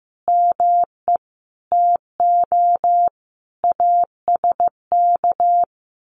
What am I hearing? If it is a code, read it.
Morse code, heard as a keyed tone, about 15 wpm, 700 Hz. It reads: ME TO ASK
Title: Sentences from Top 1500 Words 15wpm